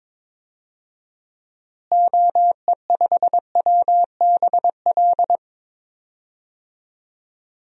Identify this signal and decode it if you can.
Morse code, heard as a keyed tone, about 22 wpm, 700 Hz. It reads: OE5WBL
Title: DXCC-100 Call Signs - 6 Characters - Repeated 2x - 22wpm